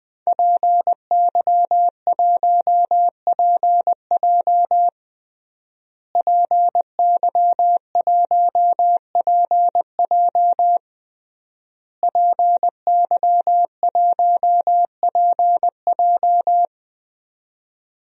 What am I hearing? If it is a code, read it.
Morse code, heard as a keyed tone, about 20 wpm, 700 Hz. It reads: PY1PJ PY1PJ PY1PJ
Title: DXCC-100 Call Signs - 5 Characters - Repeated 3x - 20wpm